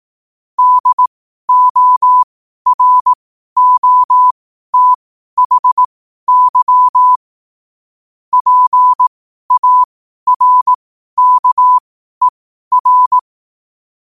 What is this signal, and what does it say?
Morse code, heard as a keyed tone, about 18 wpm, spaced as for 12 wpm, 1000 Hz. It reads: DOROTHY PARKER
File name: qotdfiles / 12wpm